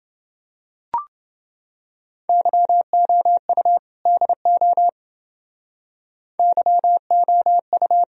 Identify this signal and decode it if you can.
Morse code, heard as a keyed tone, about 29 wpm, 700 Hz. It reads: YOU DO YOU